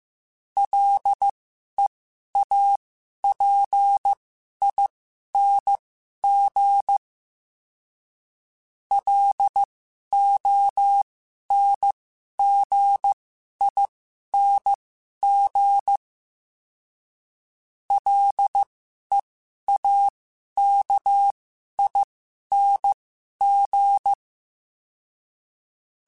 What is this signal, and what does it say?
Morse code, heard as a keyed tone, about 15 wpm, spaced as for 11 wpm, 800 Hz. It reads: LEAPING LONGING LEAKING
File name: Special INGs-mp3s / Special INGs-mp3s / Sngl Ing @10